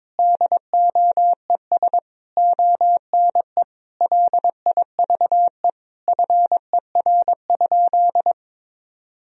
Morse code, heard as a keyed tone, about 22 wpm, 700 Hz. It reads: DOES ONE LIVE FER?
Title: Sentences from Top 200 Words and Abbreviations 22wpm